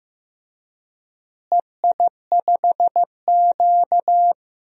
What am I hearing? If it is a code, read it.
Morse code, heard as a keyed tone, about 15 wpm, 700 Hz. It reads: EI5Q
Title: DXCC-100 Call Signs - 4 Characters - Repeated 3x - 15wpm